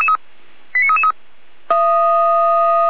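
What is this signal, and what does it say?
Five-tone selective call (ZVEI), heard as quick runs of five tones, tones of about 80 ms. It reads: 89333
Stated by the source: ZVEI five-tone selective call received on an SDR.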